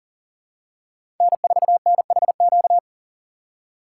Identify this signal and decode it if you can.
Morse code, heard as a keyed tone, about 40 wpm, 700 Hz. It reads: N4NHQ
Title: USA Call Signs - 5 Characters - 4x Character Spacing - 40wpm